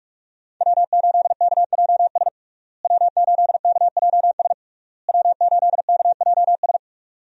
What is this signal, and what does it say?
Morse code, heard as a keyed tone, about 46 wpm, 700 Hz. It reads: W8KJS W8KJS W8KJS
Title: USA Call Signs - 5 Characters - Repeated 3x - 45wpm